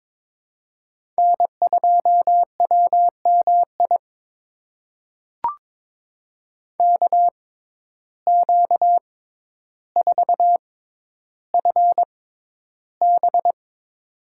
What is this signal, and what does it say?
Morse code, heard as a keyed tone, about 22 wpm, spaced as for 11 wpm, 700 Hz. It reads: N2WMI KQ4FB
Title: USA Call Signs - 5 Characters - 6x Character Spacing - 22wpm